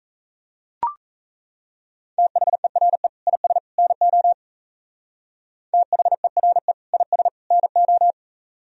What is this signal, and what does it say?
Morse code, heard as a keyed tone, about 34 wpm, 700 Hz. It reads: THERE IS NO THERE IS NO
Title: Speed-Racing - Sets of 3 Words 28wpm